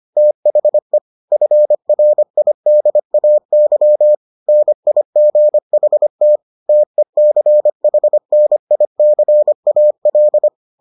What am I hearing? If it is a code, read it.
Morse code, heard as a keyed tone, about 25 wpm, 600 Hz. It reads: THE FRIDAY NIGHT TECHNICAL